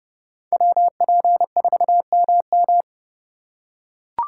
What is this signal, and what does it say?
Morse code, heard as a keyed tone, about 29 wpm, 700 Hz. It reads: WP4MM